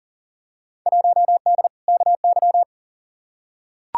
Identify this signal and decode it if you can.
Morse code, heard as a keyed tone, about 39 wpm, 700 Hz. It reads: 1D KY
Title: Field Day Exchanges - 40wpm